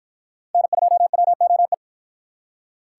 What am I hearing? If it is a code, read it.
Morse code, heard as a keyed tone, about 49 wpm, 700 Hz. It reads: N2WOE